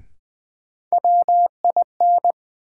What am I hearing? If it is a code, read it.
Morse code, heard as a keyed tone, about 21 wpm, 700 Hz. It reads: WIN
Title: Sentences from Top 1500 Words - New Word Drill - 20wpm